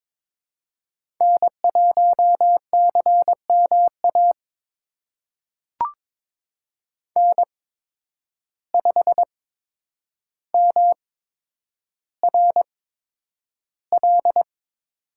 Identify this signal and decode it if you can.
Morse code, heard as a keyed tone, about 22 wpm, spaced as for 9 wpm, 700 Hz. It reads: N1CMA N5MRL
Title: USA Call Signs - 5 Characters - 8x Character Spacing - 22wpm